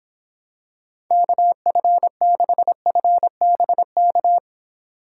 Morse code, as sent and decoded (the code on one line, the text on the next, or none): -.- ..-. -.... ..-. -... -.-
KF6FBK